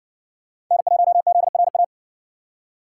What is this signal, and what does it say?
Morse code, heard as a keyed tone, about 56 wpm, 700 Hz. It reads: N1ZRA